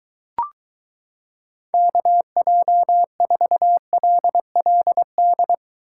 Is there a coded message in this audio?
Yes: Morse code from 1.7 to 5.5 s, heard as a keyed tone, about 23 wpm, 700 Hz. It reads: KJ4LLD